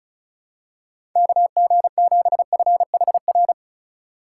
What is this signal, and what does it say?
Morse code, heard as a keyed tone, about 36 wpm, 690 Hz. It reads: KG7FHR
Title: USA Call Signs - 3x Character Spacing - 35wpm